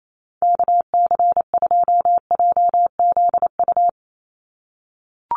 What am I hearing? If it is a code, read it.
Morse code, heard as a keyed tone, about 28 wpm, 700 Hz. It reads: KC2JZU
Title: USA Call Signs 28wpm